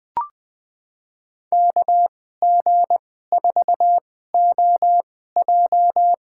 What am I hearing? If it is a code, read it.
Morse code, heard as a keyed tone, about 20 wpm, spaced as for 14 wpm, 700 Hz. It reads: KG4OJ
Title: USA Call Signs - 2x Character Spacing - 20wpm